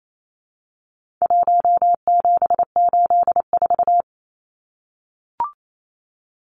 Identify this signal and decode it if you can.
Morse code, heard as a keyed tone, about 28 wpm, 700 Hz. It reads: 1784
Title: Numbers - 4-Digit 28wpm